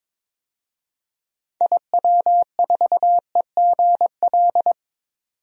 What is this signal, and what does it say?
Morse code, heard as a keyed tone, about 22 wpm, 700 Hz. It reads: IW4EGL
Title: DXCC-100 Call Signs - 6 Characters - 22wpm